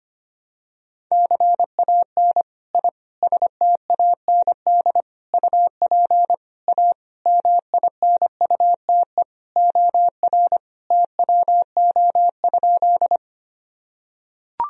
Morse code, heard as a keyed tone, about 25 wpm, 700 Hz. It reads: CAN I STAND UP A MINUTE OR TWO?
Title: Sentences from Top 1000 Words - Review Part 1 - 25wpm